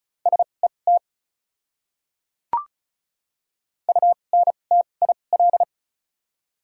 Morse code, heard as a keyed tone, about 36 wpm, spaced as for 25 wpm, 700 Hz. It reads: SET UNTIL
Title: 2x Character Spacing - Top 300 Words - 35wpm